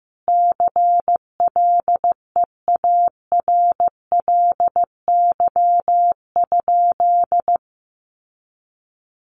Morse code, heard as a keyed tone, about 15 wpm, 700 Hz. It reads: CLEARLY?
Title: Sentences from Top 1300 Words 15wpm